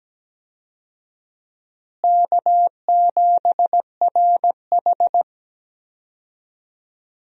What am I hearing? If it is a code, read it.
Morse code, heard as a keyed tone, about 17 wpm, 700 Hz. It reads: K7RH
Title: USA Call Signs - 4 Characters - Repeated 3x - 17wpm